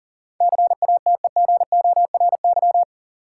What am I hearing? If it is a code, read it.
Morse code, heard as a keyed tone, about 40 wpm, 700 Hz. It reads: CATEGORY
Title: Sentences from Top 1000 Words - Review Part 7 - New Word Drill - 40wpm